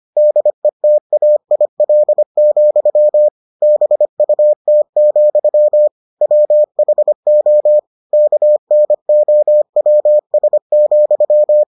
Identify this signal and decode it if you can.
Morse code, heard as a keyed tone, about 25 wpm, 600 Hz. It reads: DETAIL, BUT, WHO KNOWS,